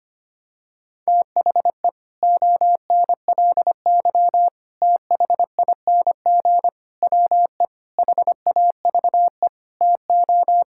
Morse code, heard as a keyed tone, about 25 wpm, 700 Hz. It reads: THE ONLY THING WE HAVE TO
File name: practice / 25wpm / Sentences with 9-Word Phrases - 25wpm